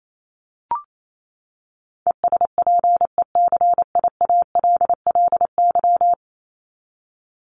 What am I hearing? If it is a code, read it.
Morse code, heard as a keyed tone, about 28 wpm, 700 Hz. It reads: ESPECIALLY
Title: Top 501-750 Words 28wpm